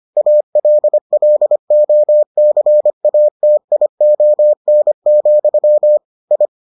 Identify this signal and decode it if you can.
Morse code, heard as a keyed tone, about 25 wpm, 600 Hz. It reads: ALLOCATION, I